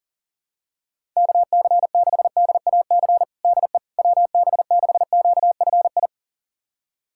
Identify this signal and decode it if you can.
Morse code, heard as a keyed tone, about 40 wpm, 700 Hz. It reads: KC6DAC DE WB6QFI
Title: USA Callsign DE Callsign - 6 Characters - 40wpm